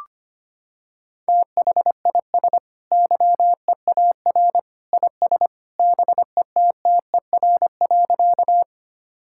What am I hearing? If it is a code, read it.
Morse code, heard as a keyed tone, about 25 wpm, 700 Hz. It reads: THIS YEAR IS BETTER.
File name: practice / 25wpm / Sentences with 2-Word Phrases - 25wpm